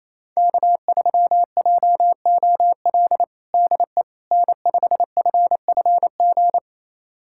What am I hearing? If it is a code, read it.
Morse code, heard as a keyed tone, about 28 wpm, 700 Hz. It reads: K3JOL DE N5FFG
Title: USA Callsign DE Callsign - 5 Characters - 28wpm